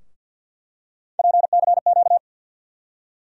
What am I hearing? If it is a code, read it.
Morse code, heard as a keyed tone, about 51 wpm, 700 Hz. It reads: PCQ
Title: USA Call Sign Postfixes - 3 Letters 50wpm